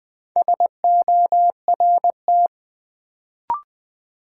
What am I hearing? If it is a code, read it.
Morse code, heard as a keyed tone, about 20 wpm, 700 Hz. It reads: SORT
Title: Sentences from Top 1300 Words - New Word Drill Alt - 20wpm